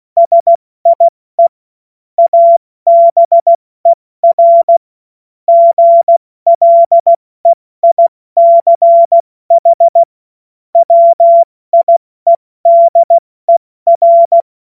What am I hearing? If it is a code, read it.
Morse code, heard as a keyed tone, about 16 wpm, 680 Hz. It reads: SIE ABER GLEICH WIEDER
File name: Morsekurs / Witze / Witz 04. April 70 F80